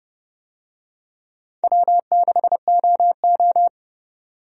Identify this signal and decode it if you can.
Morse code, heard as a keyed tone, about 31 wpm, 700 Hz. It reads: W6OO
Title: USA Call Signs - 4 Characters - 30wpm